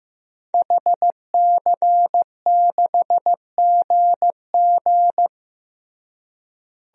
Morse code, heard as a keyed tone, about 15 wpm, 700 Hz. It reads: HC6GG
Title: DXCC-100 Call Signs - 5 Characters - Repeated 2x - 15wpm